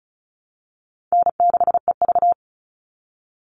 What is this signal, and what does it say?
Morse code, heard as a keyed tone, about 34 wpm, 700 Hz. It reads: N6EV
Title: Call Signs - 35wpm